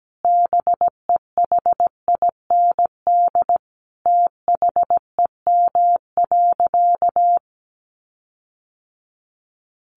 Morse code, heard as a keyed tone, about 17 wpm, 700 Hz. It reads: BEHIND THEM.
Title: Sentences from Top 1400 Words 17wpm